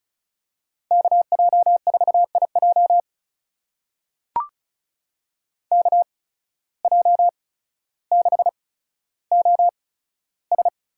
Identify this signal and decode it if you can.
Morse code, heard as a keyed tone, about 36 wpm, spaced as for 15 wpm, 700 Hz. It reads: KJ4IJ KJ6OS
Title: USA Call Signs - 5 Characters - 8x Character Spacing - 35wpm